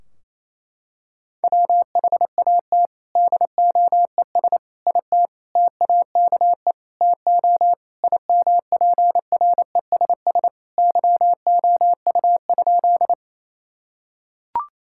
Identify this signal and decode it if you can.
Morse code, heard as a keyed tone, about 28 wpm, 700 Hz. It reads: WHAT DOES IT TAKE TO IMPRESS YOU?
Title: Sentences from Top 1600 Words 28wpm